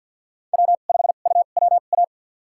Morse code, heard as a keyed tone, about 48 wpm, spaced as for 35 wpm, 700 Hz. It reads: W5UWA
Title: USA Call Signs - 5 Characters - 2x Character Spacing - 50wpm